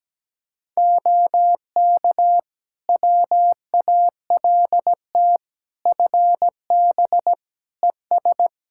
Morse code, heard as a keyed tone, about 17 wpm, 700 Hz. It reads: OK WALT FB ES
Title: Practice QSO - Exchange 2 - Info - 17wpm